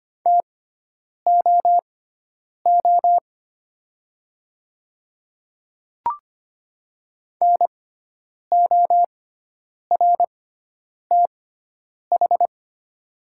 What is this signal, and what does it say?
Morse code, heard as a keyed tone, about 25 wpm, spaced as for 9 wpm, 700 Hz. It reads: TOO NORTH
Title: 6x Character Spacing - Top 300 Words - 25wpm